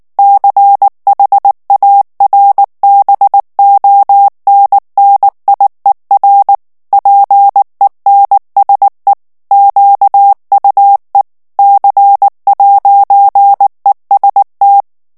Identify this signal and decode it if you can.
Morse code, heard as a keyed tone, about 19 wpm, 800 Hz. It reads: CHARBONNIER PENSE QUE C'EST
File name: FRANCAIS TEXTE CLAIR 1 VITESSE 18 MOTS MINUTES